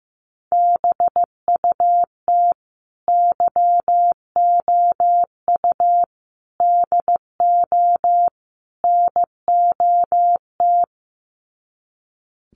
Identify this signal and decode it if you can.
Morse code, heard as a keyed tone, about 15 wpm, 700 Hz. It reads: BUT YOU DO NOT
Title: Sets of 4 Words 15wpm